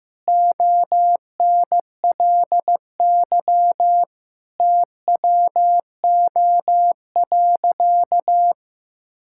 Morse code, heard as a keyed tone, about 15 wpm, 700 Hz. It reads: ONLY TWO.
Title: Sentences from Top 1000 Words - Review Part 5 - 15wpm